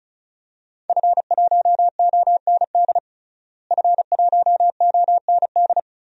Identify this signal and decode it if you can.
Morse code, heard as a keyed tone, about 36 wpm, 700 Hz. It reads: F1OND F1OND
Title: DXCC-100 Call Signs - 5 Characters - Repeated 2x - 35wpm